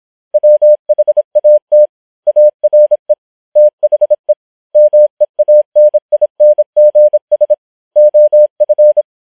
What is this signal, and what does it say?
Morse code, heard as a keyed tone, about 26 wpm, 600 Hz. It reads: WHAT ARE THE MEANINGS OF
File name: challenge193